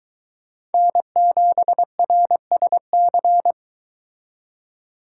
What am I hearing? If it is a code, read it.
Morse code, heard as a keyed tone, about 23 wpm, 700 Hz. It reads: N7RSC